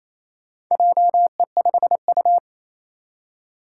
Morse code, heard as a keyed tone, about 28 wpm, 700 Hz. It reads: JE5U
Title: Four Challenging Letter-Number Combinations 28wpm